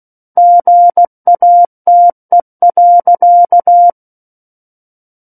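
Morse code, heard as a keyed tone, about 16 wpm, 700 Hz. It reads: GATE.
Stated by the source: Morse practice of book text at 16 wpm.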